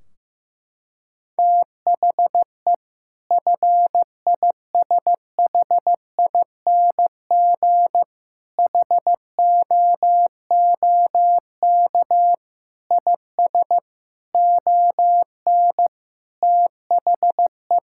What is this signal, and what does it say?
Morse code, heard as a keyed tone, about 15 wpm, 700 Hz. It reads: THE FISHING HOOK IS ON THE
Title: Sentences from Top 1600 Words 15wpm